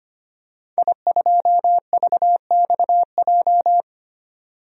Morse code, heard as a keyed tone, about 25 wpm, 700 Hz. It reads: I2VXJ
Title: DXCC-100 Call Signs - 5 Characters - 25wpm